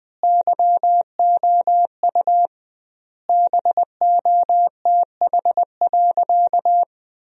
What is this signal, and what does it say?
Morse code, heard as a keyed tone, about 20 wpm, 700 Hz. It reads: YOU BOTH.
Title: 2x Word-Spacing - Sentences from Top 300 Words - 20wpm